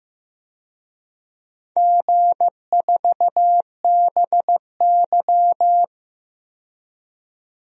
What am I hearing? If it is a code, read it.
Morse code, heard as a keyed tone, about 15 wpm, 700 Hz. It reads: G4BY
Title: DXCC-100 Call Signs - 4 Characters - 15wpm